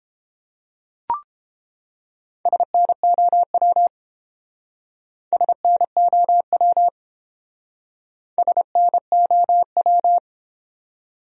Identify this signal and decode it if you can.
Morse code, heard as a keyed tone, about 28 wpm, 700 Hz. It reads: SNOW SNOW SNOW
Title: Speed-Racing Top 500 Words 22wpm